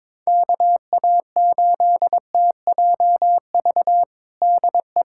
Morse code, heard as a keyed tone, about 21 wpm, 700 Hz. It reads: KA8TJV DE
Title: USA Callsign DE Callsign 22wpm